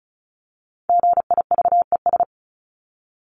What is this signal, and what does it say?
Morse code, heard as a keyed tone, about 34 wpm, 700 Hz. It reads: GIVES
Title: ES:S Suffix - Top 100 Words with Base Words 35wpm